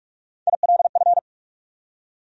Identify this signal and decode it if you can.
Morse code, heard as a keyed tone, about 44 wpm, 700 Hz. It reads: ILF
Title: USA Call Sign Postfixes - 3 Letters 45wpm